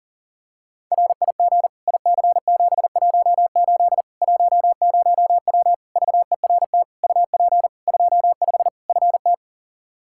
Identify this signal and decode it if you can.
Morse code, heard as a keyed tone, about 40 wpm, 700 Hz. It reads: RIG IC718 10W VERT UP 25 FT
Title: Practice QSO - Exchange 2 - Info - 40wpm